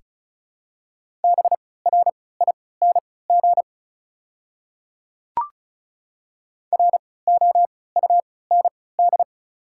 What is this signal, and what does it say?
Morse code, heard as a keyed tone, about 34 wpm, spaced as for 20 wpm, 700 Hz. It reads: BRING ROUND
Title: 3x Character Spacing - Top 500 Words - 35wpm